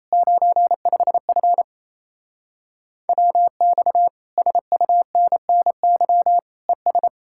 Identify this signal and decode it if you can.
Morse code, heard as a keyed tone, about 37 wpm, 700 Hz. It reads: 95F WX SUNNY ES